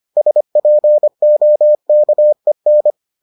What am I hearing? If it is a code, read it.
Morse code, heard as a keyed tone, about 25 wpm, 600 Hz. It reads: SPOKEN